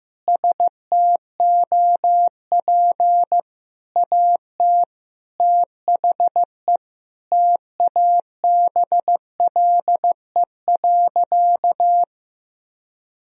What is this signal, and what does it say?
Morse code, heard as a keyed tone, about 15 wpm, 700 Hz. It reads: STOP AT THE TABLE.
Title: Sentences from Top 1000 Words - Review Part 2 - 15wpm